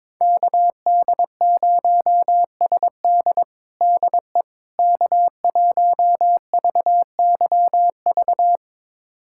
Morse code, heard as a keyed tone, about 22 wpm, 700 Hz. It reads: KD0SD DE K1VYV
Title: USA Callsign DE Callsign 22wpm